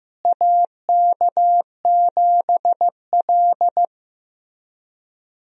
Morse code, heard as a keyed tone, about 15 wpm, 700 Hz. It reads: AK7L
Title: USA Call Signs - 4 Characters - Repeated 3x - 15wpm